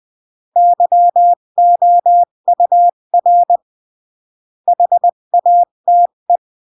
Morse code, heard as a keyed tone, about 20 wpm, 700 Hz. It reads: YOUR HATE
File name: CW / jodorowsky / 82 commandments 20-15